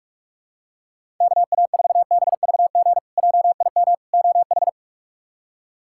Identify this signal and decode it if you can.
Morse code, heard as a keyed tone, about 41 wpm, 700 Hz. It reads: KA4BVG JIM OH
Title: K1USN SST QSO Practice - 45wpm